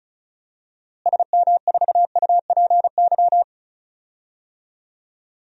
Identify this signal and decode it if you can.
Morse code, heard as a keyed tone, about 34 wpm, 700 Hz. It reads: SM4UPY
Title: DXCC-100 Call Signs - 6 Characters - Repeated 2x - 35wpm